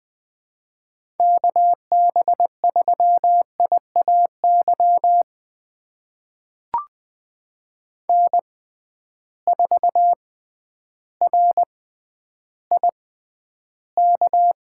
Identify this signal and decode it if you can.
Morse code, heard as a keyed tone, about 20 wpm, spaced as for 11 wpm, 700 Hz. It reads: KB3IAY N4RIK
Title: USA Call Signs - 6x Character Spacing - 20wpm